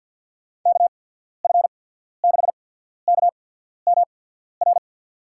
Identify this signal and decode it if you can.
Morse code, heard as a keyed tone, about 52 wpm, spaced as for 14 wpm, 700 Hz. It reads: KF6KMR
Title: USA Call Signs - 6 Characters - 8x Character Spacing - 50wpm